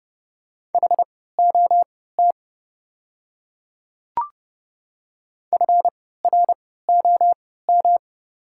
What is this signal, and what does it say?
Morse code, heard as a keyed tone, about 32 wpm, spaced as for 17 wpm, 690 Hz. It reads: HOT FROM
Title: 3x Character Spacing - Top 100 Words - 30wpm